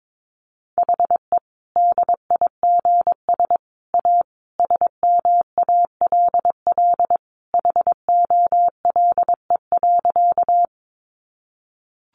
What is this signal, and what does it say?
Morse code, heard as a keyed tone, about 22 wpm, 700 Hz. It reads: HE DIGS A SMALL HOLE.
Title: Easy Phrases 22wpm